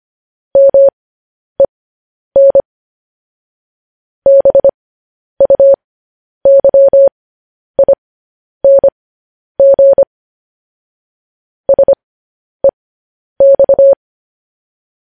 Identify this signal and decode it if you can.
Morse code, heard as a keyed tone, about 25 wpm, spaced as for 10 wpm, 550 Hz. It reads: MEN BUYING SEX
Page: news-headlines-in-morse-code-at-10-wpm